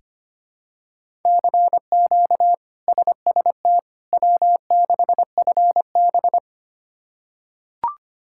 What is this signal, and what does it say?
Morse code, heard as a keyed tone, about 25 wpm, 700 Hz. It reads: CQ SST W6FB
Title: K1USN SST QSO Practice - 25wpm